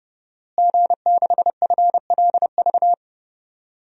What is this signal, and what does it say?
Morse code, heard as a keyed tone, about 30 wpm, 700 Hz. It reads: G6FLV
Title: DXCC-100 Call Signs - 5 Characters - Repeated 3x - 30wpm